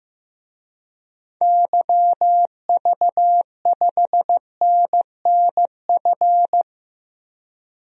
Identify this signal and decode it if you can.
Morse code, heard as a keyed tone, about 15 wpm, 700 Hz. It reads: YV5NNF
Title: DXCC-100 Call Signs - 6 Characters - Repeated 3x - 15wpm